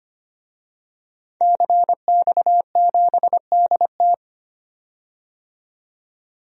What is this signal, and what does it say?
Morse code, heard as a keyed tone, about 25 wpm, 700 Hz. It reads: CX7DT
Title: DXCC-100 Call Signs - 5 Characters - 25wpm